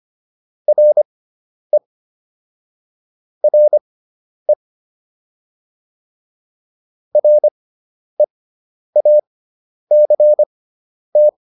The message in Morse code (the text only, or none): RE RE REACT